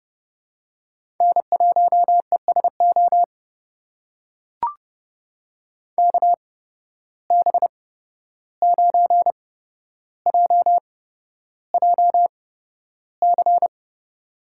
Morse code, heard as a keyed tone, about 30 wpm, spaced as for 12 wpm, 700 Hz. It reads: N1ESO KB9JJC
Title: USA Call Signs - 8x Character Spacing - 30wpm